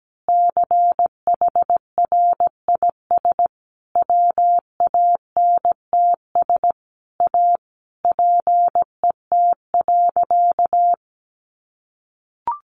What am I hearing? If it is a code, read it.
Morse code, heard as a keyed tone, about 17 wpm, 700 Hz. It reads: CHRIS WANTS A PET.
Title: Additional Easy Phrases 17wpm